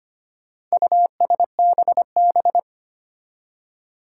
Morse code, heard as a keyed tone, about 25 wpm, 700 Hz. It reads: USBB
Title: Four Challenging Letter-Number Combinations 25wpm